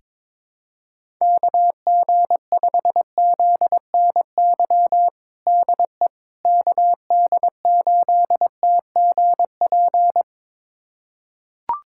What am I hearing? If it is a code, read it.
Morse code, heard as a keyed tone, about 22 wpm, 700 Hz. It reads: KG5ZNY DE KD8TGP
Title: USA Callsign DE Callsign - 6 Characters - 22wpm